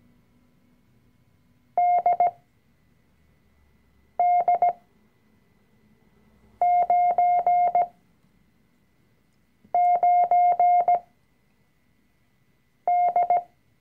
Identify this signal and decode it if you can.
Morse code, heard as a keyed tone, about 17 wpm, spaced as for 4 wpm, 710 Hz. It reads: DD99D
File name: CW-Online-Kurs / Anfänger / Lektion 005